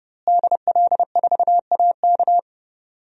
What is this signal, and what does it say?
Morse code, heard as a keyed tone, about 31 wpm, 700 Hz. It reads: DL4AK
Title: DXCC-100 Call Signs - 5 Characters - Repeated 2x - 30wpm